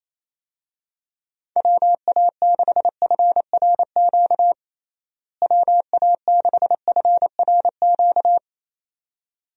Morse code, heard as a keyed tone, about 28 wpm, 700 Hz. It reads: WA6FRQ WA6FRQ
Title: USA Call Signs - 6 Characters - Repeated 2x - 28wpm